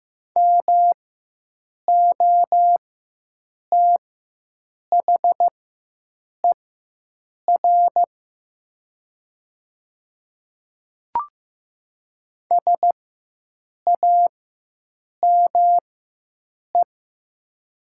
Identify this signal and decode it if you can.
Morse code, heard as a keyed tone, about 15 wpm, spaced as for 7 wpm, 700 Hz. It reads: MOTHER SAME
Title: 4x Character Spacing - Top 500 Words - 15wpm